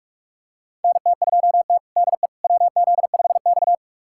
Morse code, heard as a keyed tone, about 42 wpm, 700 Hz. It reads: NT2T DE WZ5X
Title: USA Callsign DE Callsign - 4 Characters - 45wpm